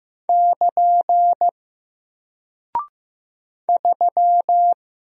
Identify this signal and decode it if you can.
Morse code, heard as a keyed tone, about 15 wpm, 700 Hz. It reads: <KN> 3
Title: Single Character (Letter:Number:Punctuation:Prosign) Alt 15wpm